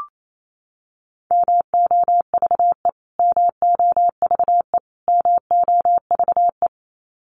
Morse code, heard as a keyed tone, about 28 wpm, 700 Hz. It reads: MOVE MOVE MOVE
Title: Top 300 Words Repeated 28wpm